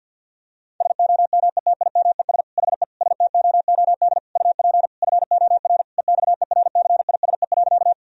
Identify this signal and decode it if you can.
Morse code, heard as a keyed tone, about 51 wpm, 700 Hz. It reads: SOMETIMES HE STOOD UP FOR EXERCISE.
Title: Sentences from Top 1000 Words - Review Part 4 - 50wpm